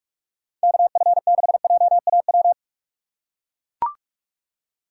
Morse code, heard as a keyed tone, about 44 wpm, 700 Hz. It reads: KF6JAW